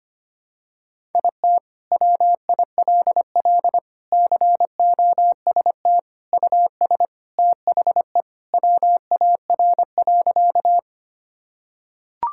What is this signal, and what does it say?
Morse code, heard as a keyed tone, about 25 wpm, 700 Hz. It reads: IT WILL COST US THE WAR.
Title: Sentences from Top 1000 Words - Review Part 5 - 25wpm